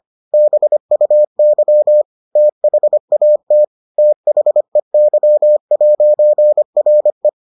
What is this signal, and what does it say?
Morse code, heard as a keyed tone, about 25 wpm, 600 Hz. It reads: BUY THAT THEY'RE